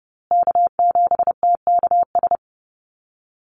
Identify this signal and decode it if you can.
Morse code, heard as a keyed tone, about 30 wpm, 700 Hz. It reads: K7TKS
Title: USA Call Signs Repeated 3x 30wpm